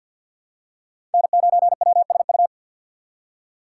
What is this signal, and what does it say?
Morse code, heard as a keyed tone, about 48 wpm, 700 Hz. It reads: N9WSU
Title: USA Call Signs - 6x Character Spacing - 50wpm